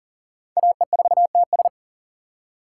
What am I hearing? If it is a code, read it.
Morse code, heard as a keyed tone, about 40 wpm, 700 Hz. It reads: AE4TS
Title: USA Call Signs - 5 Characters - 3x Character Spacing - 40wpm